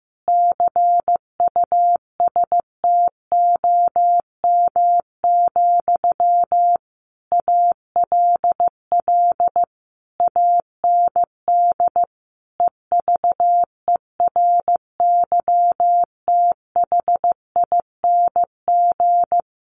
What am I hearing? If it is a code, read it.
Morse code, heard as a keyed tone, about 15 wpm, 700 Hz. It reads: CUSTOM, ALL AND EVERYTHING
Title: A Sound Of Thunder 15wpm